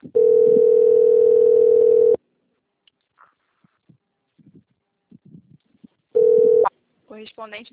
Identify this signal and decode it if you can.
Ringback tone (North America), 440 + 480 Hz, with speech.
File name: out- - - - - .6506